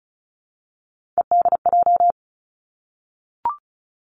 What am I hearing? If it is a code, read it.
Morse code, heard as a keyed tone, about 34 wpm, 700 Hz. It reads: EDJ
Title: Three Challenging Letter-Number Combinations 35wpm